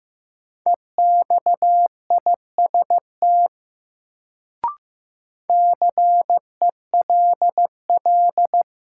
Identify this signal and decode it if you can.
Morse code, heard as a keyed tone, about 15 wpm, 700 Hz. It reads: EXIST CELL
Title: Sentences from Top 1100 Words - New Word Drill Alt - 15wpm